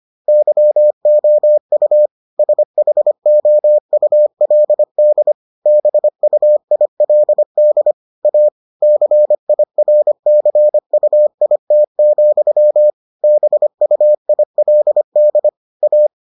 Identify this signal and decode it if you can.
Morse code, heard as a keyed tone, about 25 wpm, 600 Hz. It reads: YOU SHOULD BUILD A CIRCUIT, BUILD A